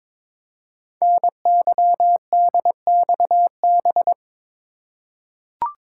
Morse code, heard as a keyed tone, about 22 wpm, 700 Hz. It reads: NYDXB
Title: Code Groups - 5 Letters - 22wpm